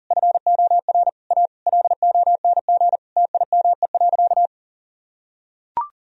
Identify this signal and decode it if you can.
Morse code, heard as a keyed tone, about 40 wpm, 700 Hz. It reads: FOR A LONG TIME.
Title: Sentences from Top 1000 Words - Review Part 2 - 40wpm